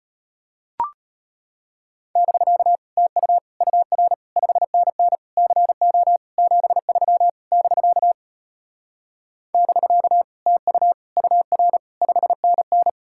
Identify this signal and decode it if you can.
Morse code, heard as a keyed tone, about 33 wpm, 700 Hz. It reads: <BK> TU UR 5NN CO 73 <BK> <BK> TU UR 5NN